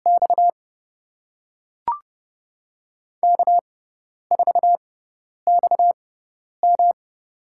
Morse code, heard as a keyed tone, about 30 wpm, 700 Hz. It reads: X K 4 X M
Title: USA Call Signs - 6x Character Spacing - 30wpm